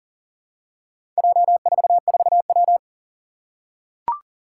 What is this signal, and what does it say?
Morse code, heard as a keyed tone, about 40 wpm, 700 Hz. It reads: J44W
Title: Four Challenging Letter-Number Combinations 40wpm